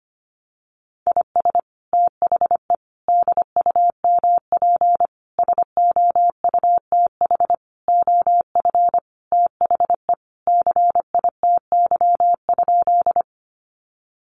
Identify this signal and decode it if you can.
Morse code, heard as a keyed tone, about 25 wpm, 700 Hz. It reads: IS THE DUMP SOUTH OF THE CITY?
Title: Sentences from Top 1300 Words 25wpm